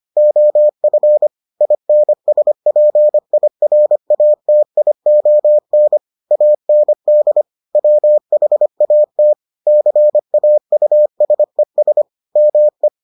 Morse code, heard as a keyed tone, about 25 wpm, 600 Hz. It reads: OF INSPIRATION AND WHAT CAUSES ME